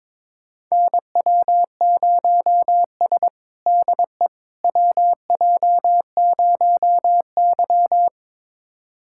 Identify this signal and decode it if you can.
Morse code, heard as a keyed tone, about 22 wpm, 700 Hz. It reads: NW0S DE WJ0Y
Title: USA Callsign DE Callsign - 4 Characters - 22wpm